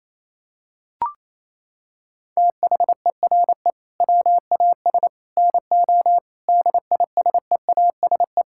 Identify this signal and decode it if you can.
Morse code, heard as a keyed tone, about 28 wpm, 700 Hz. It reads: THERE WAS NO DISEASE
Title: Sentences from Top 1000 Words - Review Part 6 - 28wpm